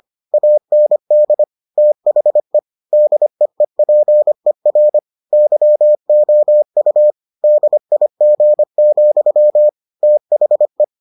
Morse code, heard as a keyed tone, about 25 wpm, 600 Hz. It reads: AND THE DEEPER YOU DIG, THE